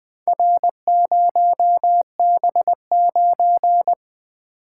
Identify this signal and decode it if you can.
Morse code, heard as a keyed tone, about 22 wpm, 700 Hz. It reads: R0B9
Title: Speed-Racing - Four Letter-Number Combinations - 20wpm